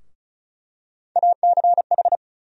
Morse code, heard as a keyed tone, about 34 wpm, 700 Hz. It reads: ACH
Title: USA Call Sign Postfixes - 3 Letters 35wpm